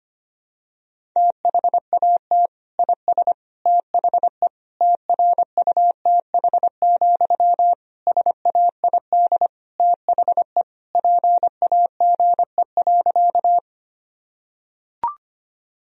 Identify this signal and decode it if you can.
Morse code, heard as a keyed tone, about 25 wpm, 700 Hz. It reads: THAT IS THE TRUTH, SAID THE PAGE.
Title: Sentences from Top 1000 Words - Review Part 1 - 25wpm